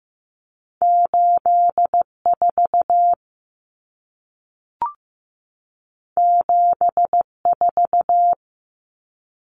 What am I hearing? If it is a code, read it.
Morse code, heard as a keyed tone, about 15 wpm, 700 Hz. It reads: 84 74
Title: Common Cut Numbers - 2-Digit 15wpm